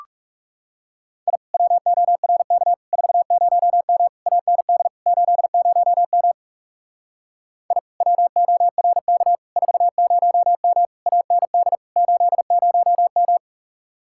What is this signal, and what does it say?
Morse code, heard as a keyed tone, about 38 wpm, 700 Hz. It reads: I WORK 40M AND 80M I WORK 40M AND 80M